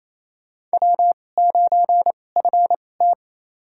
Morse code, heard as a keyed tone, about 27 wpm, spaced as for 20 wpm, 700 Hz. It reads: W9FT
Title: USA Call Signs - 4 Characters - 2x Character Spacing - 28wpm